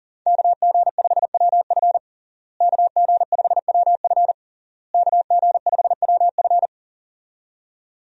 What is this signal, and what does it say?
Morse code, heard as a keyed tone, about 40 wpm, 700 Hz. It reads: KG5WF KG5WF KG5WF
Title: USA Call Signs - 5 Characters - Repeated 3x - 40wpm